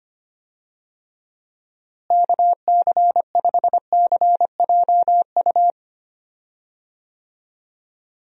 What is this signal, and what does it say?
Morse code, heard as a keyed tone, about 25 wpm, 700 Hz. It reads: KC5CJU
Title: USA Call Signs - 6 Characters - Repeated 3x - 25wpm